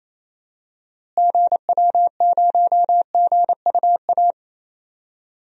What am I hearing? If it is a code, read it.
Morse code, heard as a keyed tone, about 28 wpm, 700 Hz. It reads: GW0GUA
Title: DXCC-100 Call Signs - 6 Characters - Repeated 2x - 28wpm